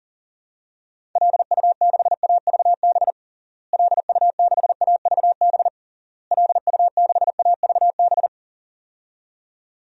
Morse code, heard as a keyed tone, about 40 wpm, 700 Hz. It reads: LU6AVB LU6AVB LU6AVB
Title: DXCC-100 Call Signs - 6 Characters - Repeated 3x - 40wpm